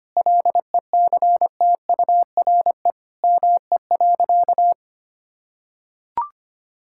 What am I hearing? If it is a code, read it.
Morse code, heard as a keyed tone, about 25 wpm, 700 Hz. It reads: LECTURE ME.
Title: Sentences from Top 1600 Words 25wpm